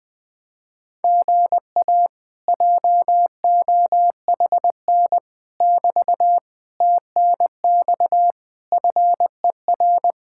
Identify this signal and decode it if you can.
Morse code, heard as a keyed tone, about 20 wpm, 700 Hz. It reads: GA JOHN <BT> TNX FER
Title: Speed-Racing Common QSO Phrases Encore - 17wpm